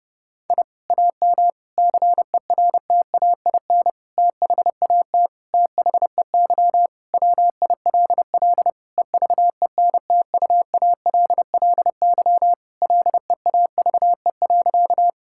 Morse code, heard as a keyed tone, about 30 wpm, 700 Hz. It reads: I AM CERTAIN THAT THEY WILL EVENTUALLY LEAVE.
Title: Sentences from Top 1000 Words - Review Part 6 - 30wpm